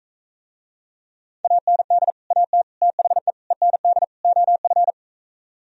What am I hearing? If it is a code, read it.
Morse code, heard as a keyed tone, about 44 wpm, 700 Hz. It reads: AND AT THE END OF